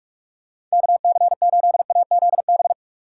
Morse code, heard as a keyed tone, about 42 wpm, 700 Hz. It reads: KC8AZB
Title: USA Call Signs - 6 Characters - 6x Character Spacing - 45wpm